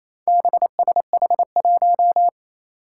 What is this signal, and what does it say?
Morse code, heard as a keyed tone, about 28 wpm, 700 Hz. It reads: BSH1
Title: Four Challenging Letter-Number Combinations 28wpm